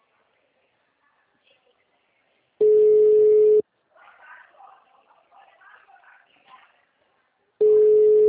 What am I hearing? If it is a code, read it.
Ringback tone (Europe, 425 Hz).